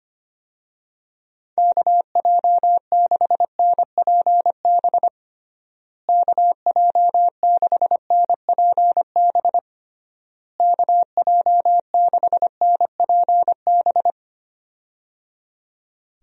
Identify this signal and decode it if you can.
Morse code, heard as a keyed tone, about 25 wpm, 700 Hz. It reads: KJ6NPB KJ6NPB KJ6NPB
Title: USA Call Signs - 6 Characters - Repeated 3x - 25wpm